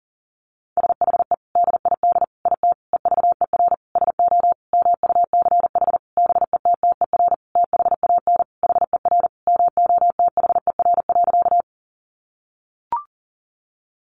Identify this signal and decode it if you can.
Morse code, heard as a keyed tone, about 40 wpm, 700 Hz. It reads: SHE DID IT EVER SO MUCH BETTER THAN HER MOTHER.
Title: Sentences from Top 400 Words 40wpm